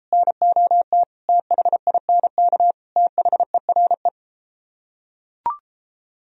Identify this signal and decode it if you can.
Morse code, heard as a keyed tone, about 33 wpm, 700 Hz. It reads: NOT THINK THERE